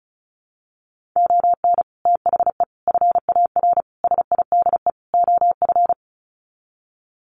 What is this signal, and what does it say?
Morse code, heard as a keyed tone, about 34 wpm, 700 Hz. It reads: ON THE FAR SIDE OF
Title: Sets of 5 Words 35wpm